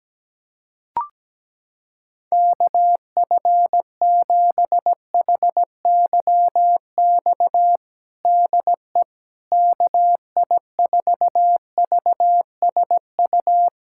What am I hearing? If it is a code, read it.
Morse code, heard as a keyed tone, about 17 wpm, 700 Hz. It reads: KF7HYX DE KI4VSU
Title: USA Callsign DE Callsign - 6 Characters - 17wpm